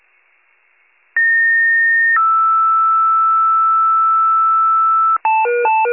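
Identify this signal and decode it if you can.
Two-tone page, heard as a one-second tone, then a three-second tone, about 1850 then 1320 Hz, with noise. It is heard from 1.2 to 5.2 s.
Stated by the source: Fire/ambulance (EMS) pager alert tones.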